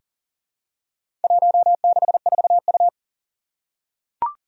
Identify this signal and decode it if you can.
Morse code, heard as a keyed tone, about 39 wpm, 700 Hz. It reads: 164U
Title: Four Challenging Letter-Number Combinations 40wpm